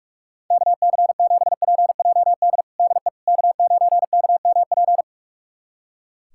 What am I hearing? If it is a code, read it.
Morse code, heard as a keyed tone, about 45 wpm, 700 Hz. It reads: KC7PJD DE K9KMP
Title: USA Callsign DE Callsign 45wpm